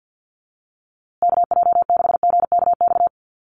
Morse code, heard as a keyed tone, about 48 wpm, 700 Hz. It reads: KJ6GKX
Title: USA Call Signs Repeated 3x 50wpm